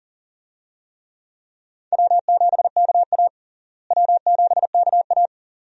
Morse code, heard as a keyed tone, about 40 wpm, 700 Hz. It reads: W7KA W7KA
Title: USA Call Signs - 4 Characters - Repeated 2x - 40wpm